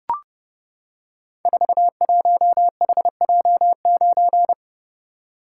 Morse code, heard as a keyed tone, about 30 wpm, 700 Hz. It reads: 41HJ9
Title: Code Groups - 5 Letter:Numbers - 30wpm